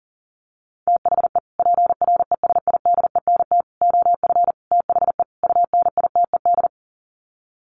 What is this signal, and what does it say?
Morse code, heard as a keyed tone, about 40 wpm, 700 Hz. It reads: THE PRESIDENT OF THE UNITED
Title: Sets of 5 Words 40wpm